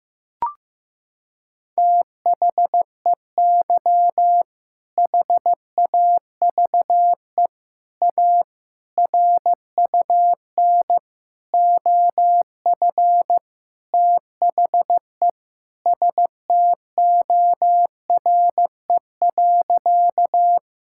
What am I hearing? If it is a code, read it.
Morse code, heard as a keyed tone, about 15 wpm, 700 Hz. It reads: THEY HAVE A RUN OF THE STORE.
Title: Sentences from Top 1000 Words - Review Part 5 - 15wpm